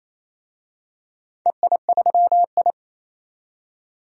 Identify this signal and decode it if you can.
Morse code, heard as a keyed tone, about 28 wpm, 700 Hz. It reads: EI3I
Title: DXCC-100 Call Signs - 4 Characters - 28wpm